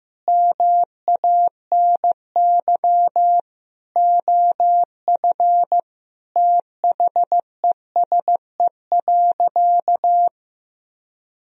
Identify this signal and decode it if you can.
Morse code, heard as a keyed tone, about 15 wpm, 700 Hz. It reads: MANY OF THESE.
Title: Sentences from Top 1000 Words - Review Part 1 - 15wpm